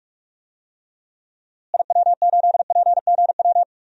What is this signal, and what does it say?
Morse code, heard as a keyed tone, about 42 wpm, 700 Hz. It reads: IW8PGW
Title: DXCC-100 Call Signs - 6 Characters - Repeated 2x - 45wpm